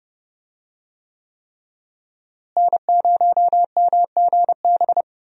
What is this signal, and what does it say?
Morse code, heard as a keyed tone, about 30 wpm, 700 Hz. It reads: N0MGB
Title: USA Call Signs - 5 Characters - Repeated 3x - 30wpm